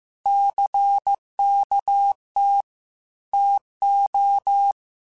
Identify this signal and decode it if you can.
Morse code, heard as a keyed tone, about 15 wpm, 800 Hz. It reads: CKT TO
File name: Technical Subjects / Technical Subjects-mp3s / Sw Pwr Sup @13